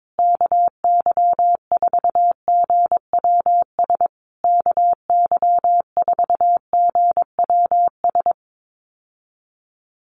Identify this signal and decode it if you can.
Morse code, heard as a keyed tone, about 22 wpm, 700 Hz. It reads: KY4GWS KY4GWS
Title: USA Call Signs Repeated 3x 22wpm